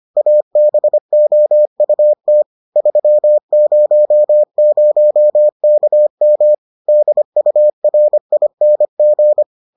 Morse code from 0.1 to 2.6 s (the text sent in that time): ABOUT